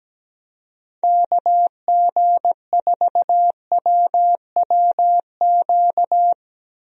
Morse code, heard as a keyed tone, about 17 wpm, 700 Hz. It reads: KG4WWQ
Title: USA Call Signs - 6 Characters - 17wpm